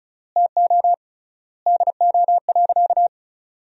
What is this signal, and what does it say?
Morse code, heard as a keyed tone, about 34 wpm, 700 Hz. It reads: TO DO.
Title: 3x Word-Spacing - Sentences from Top 100 Words - 35wpm